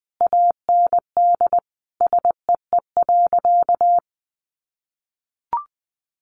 Morse code, heard as a keyed tone, about 20 wpm, 700 Hz. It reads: AND SEE.
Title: Sentences from Top 800 Words 20wpm